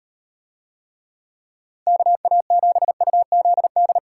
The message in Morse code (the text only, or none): KA7UZD